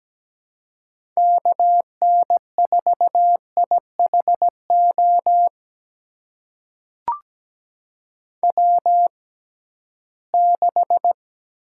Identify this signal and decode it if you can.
Morse code, heard as a keyed tone, about 17 wpm, spaced as for 13 wpm, 700 Hz. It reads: KN4IHO W6
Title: USA Call Signs - 6x Character Spacing - 17wpm